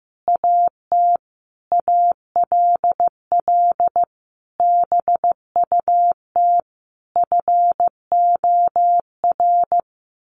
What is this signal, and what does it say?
Morse code, heard as a keyed tone, about 15 wpm, 700 Hz. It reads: AT ALL BUT FOR
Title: Sentences from Top 1400 Words 15wpm